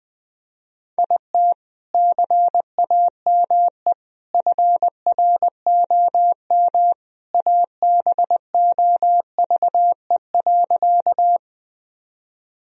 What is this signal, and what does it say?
Morse code, heard as a keyed tone, about 21 wpm, 700 Hz. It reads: IT CAME FROM ABOVE.
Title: Sentences from Top 1000 Words - Review Part 2 - 20wpm